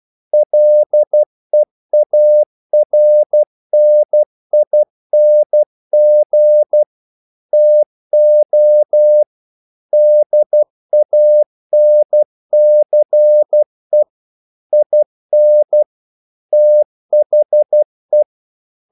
Morse code, heard as a keyed tone, about 12 wpm, 600 Hz. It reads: LEARNING TO DANCE IN THE
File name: CW / Practice / EnglishQuotations0112